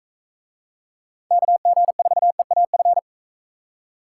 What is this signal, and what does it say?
Morse code, heard as a keyed tone, about 41 wpm, 700 Hz. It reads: KG4EAF